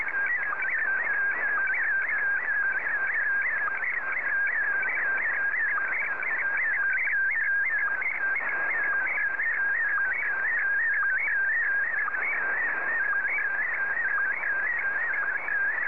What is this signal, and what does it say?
SSTV (slow-scan television), Scottie DX.